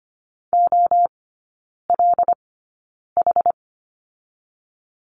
Morse code, heard as a keyed tone, about 25 wpm, spaced as for 9 wpm, 690 Hz. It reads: OLH
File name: Lesson 14e - Number 2 - Three Characters 25wpm